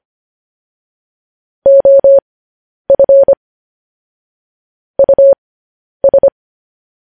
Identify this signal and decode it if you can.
Morse code, heard as a keyed tone, about 25 wpm, spaced as for 10 wpm, 550 Hz. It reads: OF US
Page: news headlines in morse code at 10 wpm